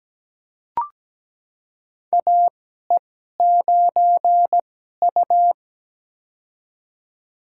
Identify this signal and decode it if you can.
Morse code, heard as a keyed tone, about 17 wpm, spaced as for 12 wpm, 700 Hz. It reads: AE9U
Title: USA Call Signs - 4 Characters - 2x Character Spacing - 17wpm